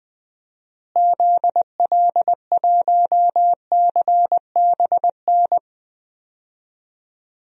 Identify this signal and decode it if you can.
Morse code, heard as a keyed tone, about 20 wpm, 700 Hz. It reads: ZL1CBN
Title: DXCC-100 Call Signs - 6 Characters - Repeated 3x - 20wpm